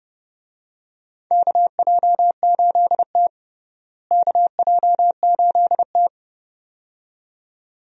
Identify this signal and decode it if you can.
Morse code, heard as a keyed tone, about 30 wpm, 700 Hz. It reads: KJ8T KJ8T
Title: USA Call Signs - 4 Characters - Repeated 2x - 30wpm